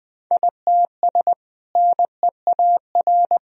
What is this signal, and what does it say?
Morse code, heard as a keyed tone, about 20 wpm, 700 Hz. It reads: ITS NEAR
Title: Sentences from Top 1000 Words - Review Part 2 - 20wpm